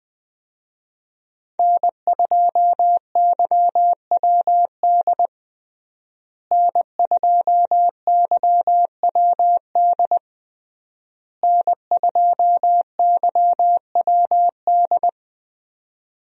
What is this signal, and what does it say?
Morse code, heard as a keyed tone, about 21 wpm, 700 Hz. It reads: N2YWD N2YWD N2YWD
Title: USA Call Signs - 5 Characters - Repeated 3x - 20wpm